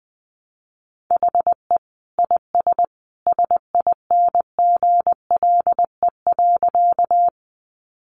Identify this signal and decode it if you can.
Morse code, heard as a keyed tone, about 20 wpm, 700 Hz. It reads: HE IS SINGLE.
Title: Sentences from Top 700 Words 20wpm